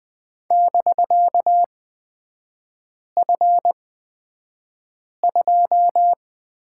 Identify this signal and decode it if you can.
Morse code, heard as a keyed tone, about 20 wpm, spaced as for 5 wpm, 700 Hz. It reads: <BK>F2
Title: 250 Lesson 40e - BK - Three Characters 20wpm